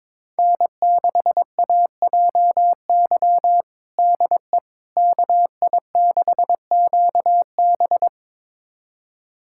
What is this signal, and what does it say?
Morse code, heard as a keyed tone, about 21 wpm, 700 Hz. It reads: N6AJY DE KI6QB
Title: USA Callsign DE Callsign - 5 Characters - 22wpm